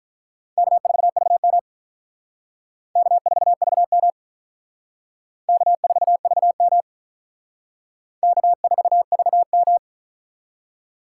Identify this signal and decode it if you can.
Morse code, heard as a keyed tone, about 43 wpm, 700 Hz. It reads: K4VM K4VM K4VM K4VM
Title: Speed Racing - USA 1x2 Call Signs 35wpm